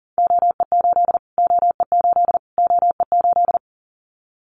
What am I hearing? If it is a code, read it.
Morse code, heard as a keyed tone, about 39 wpm, 700 Hz. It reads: OE8 OE8 OE8
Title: Callsign Prefixes for Top 25 DXCC - Repeated 3x - 40wpm